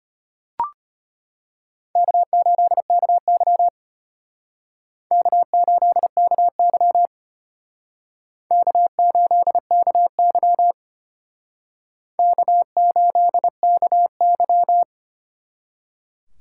Morse code, heard as a keyed tone, about 32 wpm, 700 Hz. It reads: K8KY K8KY K8KY K8KY